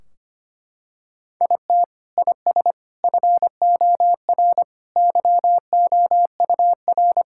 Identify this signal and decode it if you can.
Morse code, heard as a keyed tone, about 25 wpm, 700 Hz. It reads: IT IS FOR YOUR
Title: Sentences from Top 1600 Words 25wpm